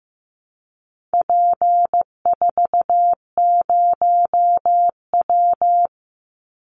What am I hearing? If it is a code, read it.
Morse code, heard as a keyed tone, about 15 wpm, 700 Hz. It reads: P40W
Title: Call Signs Repeated 3x - 15wpm